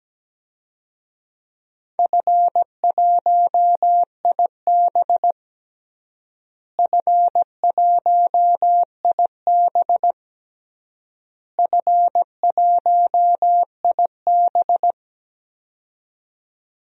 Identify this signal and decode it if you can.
Morse code, heard as a keyed tone, about 17 wpm, 700 Hz. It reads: F1IB F1IB F1IB
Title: DXCC-100 Call Signs - 4 Characters - Repeated 3x - 17wpm